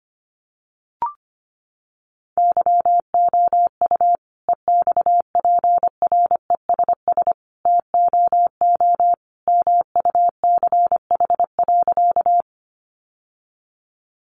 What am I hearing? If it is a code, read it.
Morse code, heard as a keyed tone, about 25 wpm, 690 Hz. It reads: YOU EXPRESS TOO MUCH.
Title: Sentences from Top 1100 Words 25wpm